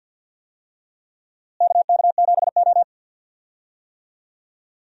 Morse code, heard as a keyed tone, about 47 wpm, 700 Hz. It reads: KK7O
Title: USA Call Signs - 4 Characters - Repeated 3x - 50wpm